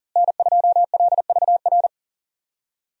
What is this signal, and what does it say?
Morse code, heard as a keyed tone, about 39 wpm, 700 Hz. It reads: N2LVR
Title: USA Call Signs - 5 Characters - Repeated 3x - 40wpm